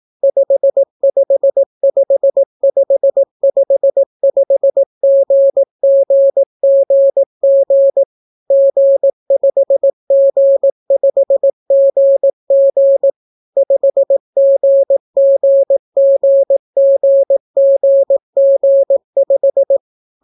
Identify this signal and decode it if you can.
Morse code, heard as a keyed tone, about 18 wpm, 550 Hz. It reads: 555555GGGG G5G5GG 5GGGGGG5